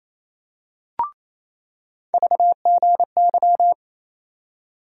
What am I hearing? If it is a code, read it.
Morse code, heard as a keyed tone, about 28 wpm, 700 Hz. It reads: VGY
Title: USA Call Sign Postfixes - 3 Letters 28wpm